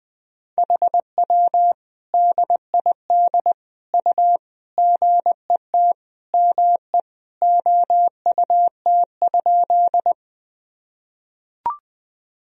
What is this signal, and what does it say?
Morse code, heard as a keyed tone, about 20 wpm, 700 Hz. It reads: HW DID U GET ME OUT?
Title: Sentences from Top 200 Words and Abbreviations 20wpm